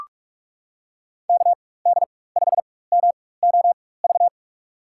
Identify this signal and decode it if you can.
Morse code, heard as a keyed tone, about 46 wpm, spaced as for 21 wpm, 700 Hz. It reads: KD5MOV
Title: USA Call Signs - 6 Characters - 4x Character Spacing - 45wpm